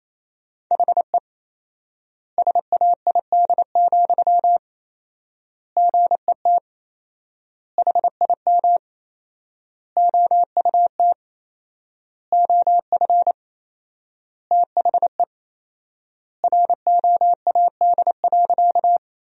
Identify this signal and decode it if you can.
Morse code, heard as a keyed tone, about 28 wpm, 700 Hz. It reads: HE SAID, GET HIM OUT OF THE ROAD.
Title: 4x Word-Spacing - Sentences from Top 400 Words - 28wpm